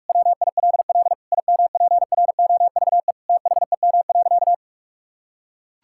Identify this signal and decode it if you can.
Morse code, heard as a keyed tone, about 44 wpm, 700 Hz. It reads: WILL IMPROVE THEM.
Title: Sentences from Top 1000 Words - Review Part 5 - 45wpm